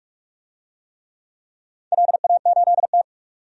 Morse code, heard as a keyed tone, about 46 wpm, 700 Hz. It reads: LA8T
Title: DXCC-100 Call Signs - 4 Characters - 45wpm